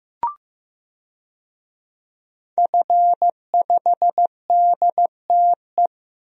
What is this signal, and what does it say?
Morse code, heard as a keyed tone, about 15 wpm, 700 Hz. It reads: F5DTE
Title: DXCC-100 Call Signs - 5 Characters - Repeated 2x - 15wpm